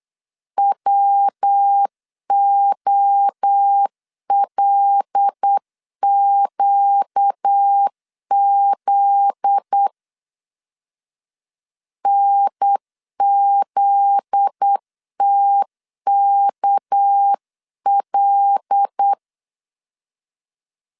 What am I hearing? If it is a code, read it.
Morse code, heard as a keyed tone, about 8 wpm, 800 Hz. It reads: WOLQZ NZTKL
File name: LEZIONI / 7 GF9Z / 7 GF9Z 8wpm 3213